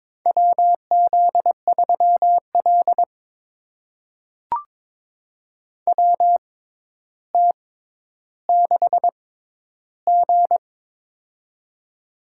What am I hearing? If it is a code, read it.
Morse code, heard as a keyed tone, about 22 wpm, spaced as for 11 wpm, 700 Hz. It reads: WZ3L WT6G
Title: USA Call Signs - 4 Characters - 6x Character Spacing - 22wpm